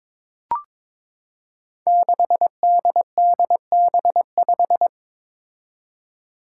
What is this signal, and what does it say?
Morse code, heard as a keyed tone, about 22 wpm, 700 Hz. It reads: 6DDB5
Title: Five Challenging Letter-Number Combinations 22wpm